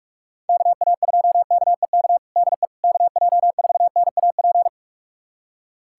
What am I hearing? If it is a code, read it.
Morse code, heard as a keyed tone, about 46 wpm, 700 Hz. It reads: KA2KEK DE KJ4NAP
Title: USA Callsign DE Callsign - 6 Characters - 45wpm